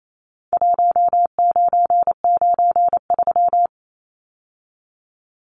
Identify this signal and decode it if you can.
Morse code, heard as a keyed tone, about 28 wpm, 700 Hz. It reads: 1993
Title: Numbers - Contest Style - 4-Digit 28wpm